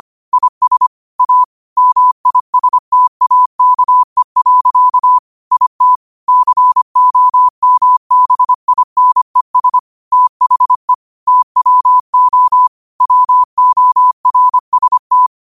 Morse code, heard as a keyed tone, about 25 wpm, 1000 Hz. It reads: IS A MISTAKE. IT COMBINES THE TWO WORST
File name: qotdfiles / 25wpm